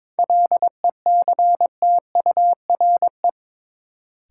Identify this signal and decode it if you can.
Morse code, heard as a keyed tone, about 22 wpm, 700 Hz. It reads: LECTURE
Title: Sentences from Top 1600 Words - New Word Drill Alt - 22wpm